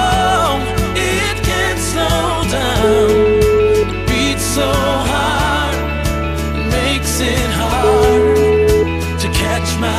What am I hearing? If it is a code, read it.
Ringback tone (Europe, 425 Hz), with noise.